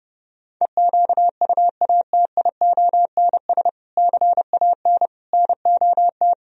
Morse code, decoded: EQUATIONS CAN NOT

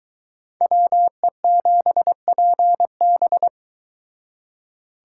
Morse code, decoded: WE7PB

